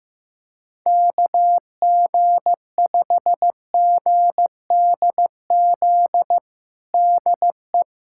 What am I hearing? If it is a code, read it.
Morse code, heard as a keyed tone, about 15 wpm, 700 Hz. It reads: KG5GDZ DE
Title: USA Callsign DE Callsign - 6 Characters - 15wpm